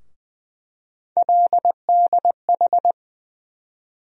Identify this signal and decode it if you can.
Morse code, heard as a keyed tone, about 20 wpm, 700 Hz. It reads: LDH
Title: USA Call Sign Postfixes - 3 Letters 20wpm